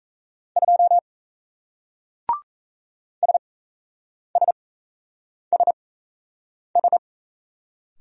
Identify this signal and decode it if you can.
Morse code, heard as a keyed tone, about 38 wpm, spaced as for 8 wpm, 700 Hz. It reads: 2 SSSS